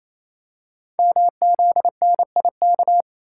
Morse code, heard as a keyed tone, about 27 wpm, 700 Hz. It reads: MZNIK
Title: Code Groups - 5 Letters - 28wpm